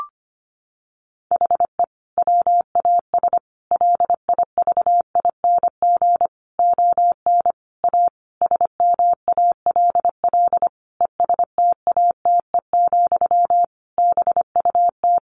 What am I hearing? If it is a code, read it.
Morse code, heard as a keyed tone, about 25 wpm, 700 Hz. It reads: HE WAS LIVING ON A SMALL ESTATE, BUT
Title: Sentences from Top 1200 Words 25wpm